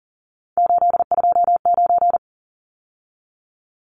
Morse code, heard as a keyed tone, about 40 wpm, 700 Hz. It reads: 829
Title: Numbers - 3-Digit 40wpm